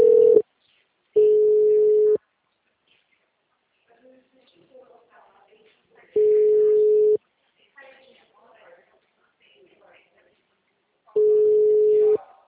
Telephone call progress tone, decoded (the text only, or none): ringback tone